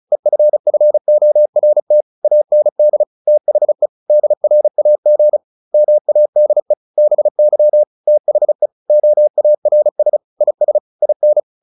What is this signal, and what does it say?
Morse code, heard as a keyed tone, about 34 wpm, 600 Hz. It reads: EFFORT AND THE DRAG MADE BY THE OARS IS IN
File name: the old man and the sea 2 35wpm